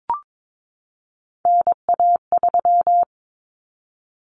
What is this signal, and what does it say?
Morse code, heard as a keyed tone, about 22 wpm, 700 Hz. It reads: NA3
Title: Common Cut Numbers - 3-Digit 22wpm